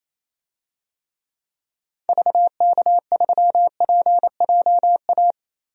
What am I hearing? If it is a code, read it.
Morse code, heard as a keyed tone, about 27 wpm, 700 Hz. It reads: VK3PJA
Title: DXCC-100 Call Signs - 6 Characters - Repeated 2x - 28wpm